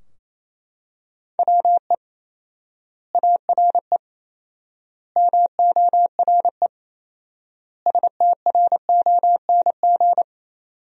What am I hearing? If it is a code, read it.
Morse code, heard as a keyed tone, about 28 wpm, 700 Hz. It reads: WE ARE MORE STRONG
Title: 4x Word-Spacing - Sentences from Top 500 Words - 28wpm